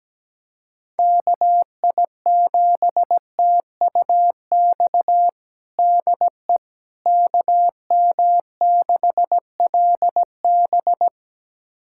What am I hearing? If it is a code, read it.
Morse code, heard as a keyed tone, about 17 wpm, 700 Hz. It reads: KI7TUX DE KM6LB
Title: USA Callsign DE Callsign 17wpm